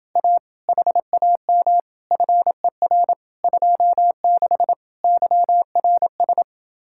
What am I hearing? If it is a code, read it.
Morse code, heard as a keyed tone, about 30 wpm, 700 Hz. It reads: A HAM FER 26 YRS